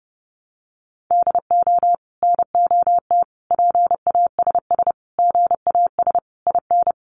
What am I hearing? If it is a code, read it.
Morse code, heard as a keyed tone, about 30 wpm, 700 Hz. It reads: DO NOT PASS GAS IN
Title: Sentences from Top 1100 Words 30wpm